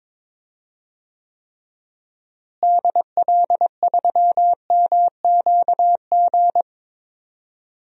Morse code, heard as a keyed tone, about 21 wpm, 700 Hz. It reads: DL3MQG
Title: DXCC-100 Call Signs - 6 Characters - Repeated 3x - 22wpm